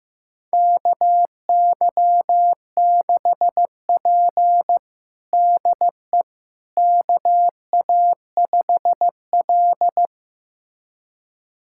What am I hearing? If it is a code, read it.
Morse code, heard as a keyed tone, about 16 wpm, 700 Hz. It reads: KY6P DE KA5L
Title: USA Callsign DE Callsign - 4 Characters - 15wpm